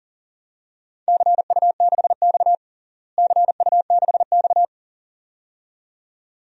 Morse code, heard as a keyed tone, about 40 wpm, 700 Hz. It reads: CU6X CU6X
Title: DXCC-100 Call Signs - 4 Characters - Repeated 2x - 40wpm